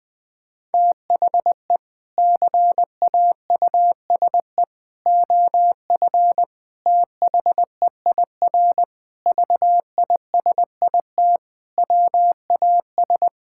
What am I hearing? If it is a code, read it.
Morse code, heard as a keyed tone, about 20 wpm, 700 Hz. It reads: THE CAUSE OF THEIR VISIT WAS
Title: Sentences from Top 1600 Words 20wpm